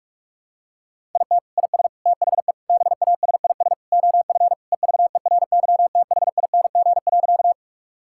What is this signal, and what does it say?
Morse code, heard as a keyed tone, about 46 wpm, 700 Hz. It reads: IT IS THE BASIS OF EVERYTHING.
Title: Sentences from Top 1000 Words - Review Part 5 - 45wpm